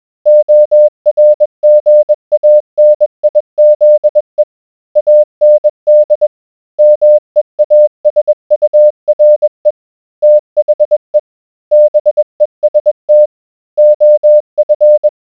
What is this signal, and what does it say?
Morse code, heard as a keyed tone, about 21 wpm, 600 Hz. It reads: ORGANIZE AND MEASURE THE BEST OF